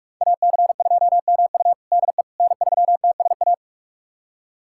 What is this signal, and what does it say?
Morse code, heard as a keyed tone, about 41 wpm, 700 Hz. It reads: AC2MU DE N3TSA
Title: USA Callsign DE Callsign - 5 Characters - 45wpm